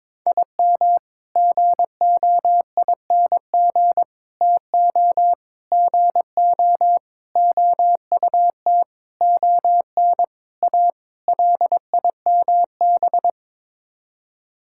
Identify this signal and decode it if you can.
Morse code, heard as a keyed tone, about 22 wpm, 700 Hz. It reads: IM GOING TO GO OUT ON A LIMB
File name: practice / 22wpm / Sets of 8 Words - 22wpm